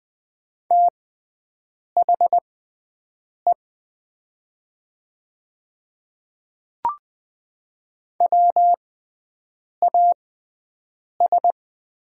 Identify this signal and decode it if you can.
Morse code, heard as a keyed tone, about 20 wpm, spaced as for 7 wpm, 700 Hz. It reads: THE WAS
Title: 6x Character Spacing - Top 100 Words - 20wpm